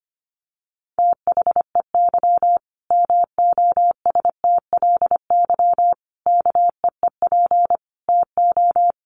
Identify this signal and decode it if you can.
Morse code, heard as a keyed tone, about 25 wpm, 700 Hz. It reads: THEY MOSTLY KEEP TO
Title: Sentences from Top 1300 Words 25wpm